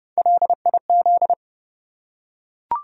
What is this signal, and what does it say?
Morse code, heard as a keyed tone, about 30 wpm, 700 Hz. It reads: LIZ
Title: USA Common Names - Max 3 Letters - 30wpm